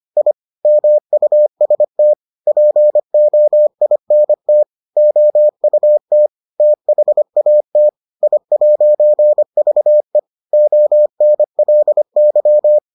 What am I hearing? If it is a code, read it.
Morse code, heard as a keyed tone, about 25 wpm, 600 Hz. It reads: I MUST POINT OUT THAT I'VE ONLY